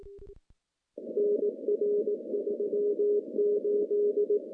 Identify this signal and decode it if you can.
Morse code, heard as a keyed tone, about 19 wpm, 410 Hz. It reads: NR38
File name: CW-SS-k5tr